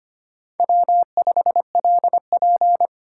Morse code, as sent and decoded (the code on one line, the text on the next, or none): .-- ..... .-.. .--.
W5LP